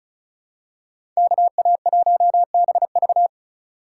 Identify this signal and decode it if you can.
Morse code, heard as a keyed tone, about 34 wpm, 700 Hz. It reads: KA1BV
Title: USA Call Signs - 5 Characters - 3x Character Spacing - 35wpm